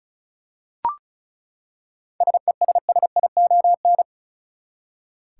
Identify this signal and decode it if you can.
Morse code, heard as a keyed tone, about 36 wpm, 700 Hz. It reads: SESSION
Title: Seven Letter Words - 35wpm